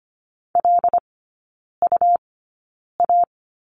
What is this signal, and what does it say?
Morse code, heard as a keyed tone, about 25 wpm, spaced as for 9 wpm, 700 Hz. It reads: LUA
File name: Lesson 17e - Letter u - Three Characters 25wpm